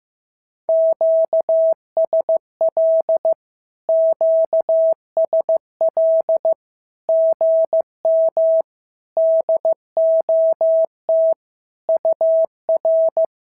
Morse code, heard as a keyed tone, about 15 wpm, 650 Hz. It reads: QSL QSL GM DOT UR
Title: SOTA QSO Practice 15wpm